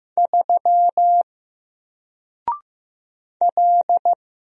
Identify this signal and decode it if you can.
Morse code, heard as a keyed tone, about 15 wpm, 700 Hz. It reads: 3 L
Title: Single Character (Letter:Number:Punctuation:Prosign) 15wpm